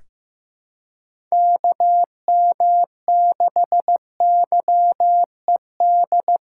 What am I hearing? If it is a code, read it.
Morse code, heard as a keyed tone, about 15 wpm, 700 Hz. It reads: KM6YED
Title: USA Callsign DE Callsign - 6 Characters - 15wpm